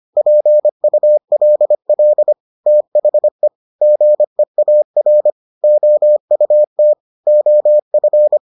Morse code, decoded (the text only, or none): PULL THE GEAR OUT OF